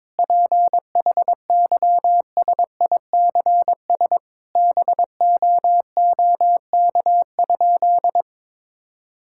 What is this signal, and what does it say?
Morse code, heard as a keyed tone, about 22 wpm, 700 Hz. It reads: PHYSICS BOOK?
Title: Sentences from Top 1000 Words - Review Part 5 - 22wpm